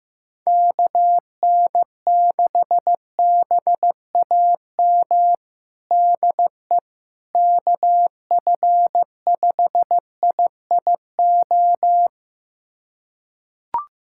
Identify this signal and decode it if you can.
Morse code, heard as a keyed tone, about 15 wpm, 700 Hz. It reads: KN6BAM DE KF5IIO
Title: USA Callsign DE Callsign - 6 Characters - 15wpm